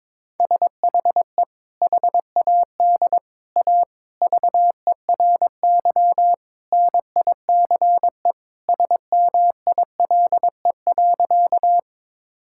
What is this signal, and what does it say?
Morse code, heard as a keyed tone, about 22 wpm, 700 Hz. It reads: SHE HAD A VERY NICE SMILE.
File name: practice / 22wpm / Sentences with 2-Word Phrases - 22wpm